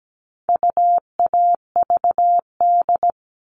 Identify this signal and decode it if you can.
Morse code, heard as a keyed tone, about 17 wpm, 700 Hz. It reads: UAVD
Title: All Cut Numbers - 4-Digit 17wpm